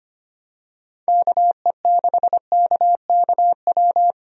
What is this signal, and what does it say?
Morse code, heard as a keyed tone, about 25 wpm, 700 Hz. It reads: KE6KKW
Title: USA Call Signs - 6 Characters - Repeated 3x - 25wpm